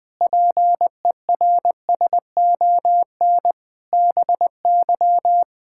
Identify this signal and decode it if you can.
Morse code, heard as a keyed tone, about 20 wpm, 700 Hz. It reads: PERSON BY
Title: Sentences from Top 1000 Words - Review Part 6 - 20wpm